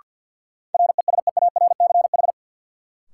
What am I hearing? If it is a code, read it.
Morse code, heard as a keyed tone, about 53 wpm, 700 Hz. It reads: RESEARCH
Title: Sentences from Top 1400 Words - New Word Drill Alt - 50wpm